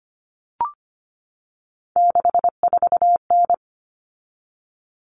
Morse code, heard as a keyed tone, about 25 wpm, 700 Hz. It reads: 64N
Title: Common Cut Numbers - 3-Digit 25wpm